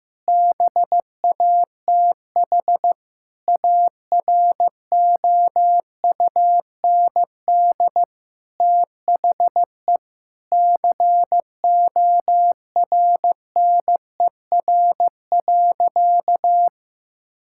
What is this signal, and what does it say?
Morse code, heard as a keyed tone, about 15 wpm, 700 Hz. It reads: BATH AROUND THE CORNER.
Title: Sentences from Top 1600 Words 15wpm